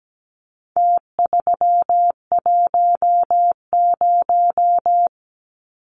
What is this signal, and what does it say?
Morse code, heard as a keyed tone, about 17 wpm, 700 Hz. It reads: T310
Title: Numbers - Contest Style - 4-Digit 17wpm